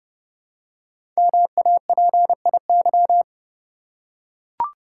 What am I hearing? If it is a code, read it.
Morse code, heard as a keyed tone, about 30 wpm, 700 Hz. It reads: MAPIY
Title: Code Groups - 5 Letters - 30wpm